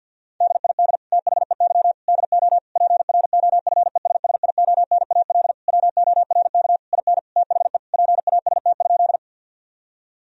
Morse code, heard as a keyed tone, about 50 wpm, 700 Hz. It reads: DID THEY DO PROFESSIONAL WORK IN THE PAST?
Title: Sentences from Top 1000 Words - Review Part 4 - 50wpm